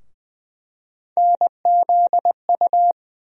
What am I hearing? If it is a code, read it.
Morse code, heard as a keyed tone, about 20 wpm, 700 Hz. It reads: NZU
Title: USA Call Sign Postfixes - 3 Letters 20wpm